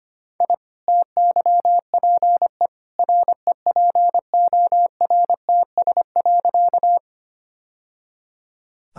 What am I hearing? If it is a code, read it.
Morse code, heard as a keyed tone, about 25 wpm, 690 Hz. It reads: I TYPE REPORTS.
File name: Easy Sentences with Hints - 25wpm